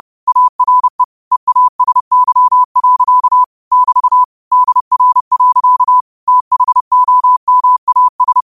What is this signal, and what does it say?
Morse code, heard as a keyed tone, about 30 wpm, 1000 Hz. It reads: ARE EASY. <BT> DR. THOMAS